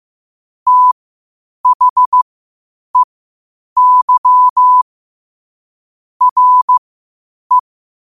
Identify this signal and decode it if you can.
Morse code, heard as a keyed tone, about 15 wpm, spaced as for 8 wpm, 1000 Hz. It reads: THEY RE
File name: qotdfiles / 8wpm